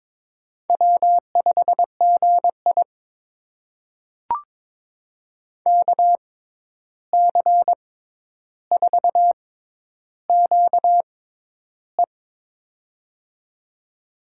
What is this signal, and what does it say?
Morse code, heard as a keyed tone, about 23 wpm, spaced as for 11 wpm, 700 Hz. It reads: W5GI KC4QE
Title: USA Call Signs - 6x Character Spacing - 22wpm